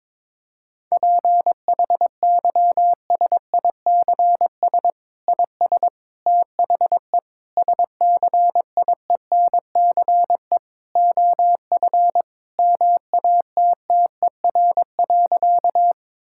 Morse code, heard as a keyed tone, about 22 wpm, 700 Hz. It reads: PHYSICS IS THE SCIENCE OF MATTER.
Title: Sentences from Top 1000 Words - Review Part 5 - 22wpm